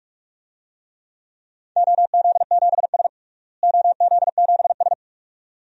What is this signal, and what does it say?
Morse code, heard as a keyed tone, about 46 wpm, 700 Hz. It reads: OZ7S OZ7S
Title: DXCC-100 Call Signs - 4 Characters - Repeated 2x - 45wpm